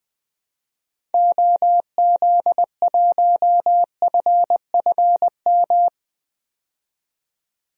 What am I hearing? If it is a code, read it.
Morse code, heard as a keyed tone, about 20 wpm, 700 Hz. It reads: OZ1FFM
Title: DXCC-100 Call Signs - 6 Characters - 20wpm